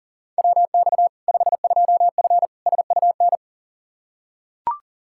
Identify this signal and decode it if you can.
Morse code, heard as a keyed tone, about 40 wpm, 700 Hz. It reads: WX 52F SUN
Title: Practice QSO - Exchange 2 - Info - 40wpm